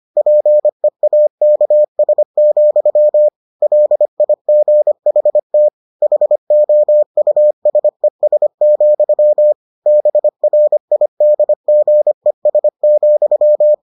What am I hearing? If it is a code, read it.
Morse code, heard as a keyed tone, about 25 wpm, 600 Hz. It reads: PEAKS, LIGHTHOUSES, BRIDGES,